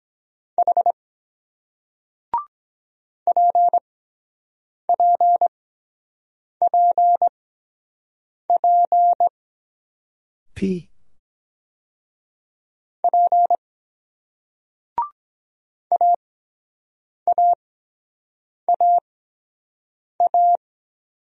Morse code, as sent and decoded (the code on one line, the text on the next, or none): .... / .--. .--. .--. .--. / .--. / .- .- .- .-
H PPPP P AAAA